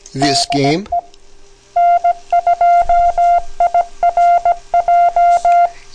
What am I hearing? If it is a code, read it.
Morse code, heard as a keyed tone, about 17 wpm, 700 Hz, with speech. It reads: DE N2IRJ
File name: 3995 kHz, LSB